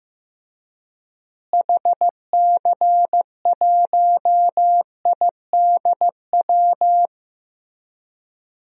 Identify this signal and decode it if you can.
Morse code, heard as a keyed tone, about 15 wpm, 700 Hz. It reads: HC1IDW
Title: DXCC-100 Call Signs - 6 Characters - Repeated 3x - 15wpm